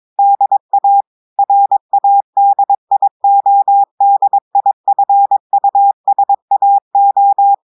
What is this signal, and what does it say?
Morse code, heard as a keyed tone, about 22 wpm, 800 Hz. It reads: DA RADIODIFUSAO
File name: audios / portugais / portugais texte en clair 1 vitesse 22 mots minutes